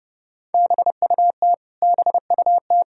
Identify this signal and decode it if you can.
Morse code, heard as a keyed tone, about 30 wpm, 700 Hz. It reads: BUT BUT
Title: Top 100 Words Repeated 30wpm